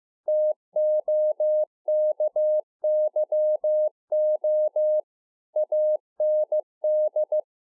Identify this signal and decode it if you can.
Morse code, heard as a keyed tone, about 15 wpm, 610 Hz. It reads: TOKYO AND
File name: 2011-04-mp3 / THE 3.11 6 15WPM